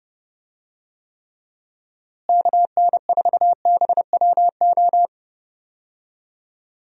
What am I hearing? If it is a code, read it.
Morse code, heard as a keyed tone, about 30 wpm, 700 Hz. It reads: KN4BWO
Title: USA Call Signs - 6 Characters - Repeated 2x - 30wpm